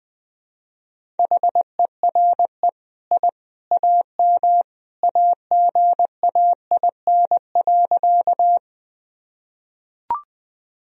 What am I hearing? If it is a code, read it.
Morse code, heard as a keyed tone, about 20 wpm, 700 Hz. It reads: HERE I AM AGAIN.
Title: Sentences from Top 1000 Words - Review Part 2 - 20wpm